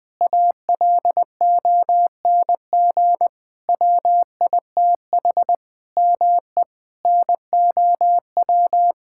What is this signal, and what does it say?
Morse code, heard as a keyed tone, about 20 wpm, 700 Hz. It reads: ALONG WITH ME NOW
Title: Sentences from Top 1000 Words - Review Part 6 - 20wpm